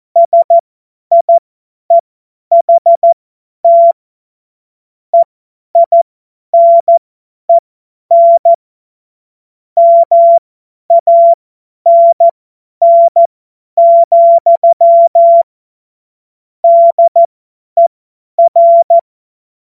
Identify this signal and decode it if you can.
Morse code, heard as a keyed tone, about 14 wpm, spaced as for 10 wpm, 680 Hz. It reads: SIEHT EINEN MANN, DER
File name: Morsekurs / Witze / Witz 15. Februar 2022 50 F70